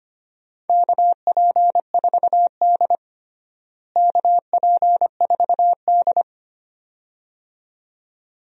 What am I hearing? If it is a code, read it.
Morse code, heard as a keyed tone, about 25 wpm, 700 Hz. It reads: KP4D KP4D
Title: DXCC-100 Call Signs - 4 Characters - Repeated 2x - 25wpm